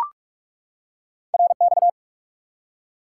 Morse code, heard as a keyed tone, about 45 wpm, 700 Hz. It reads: RX
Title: Top 180 QSO Elements 45wpm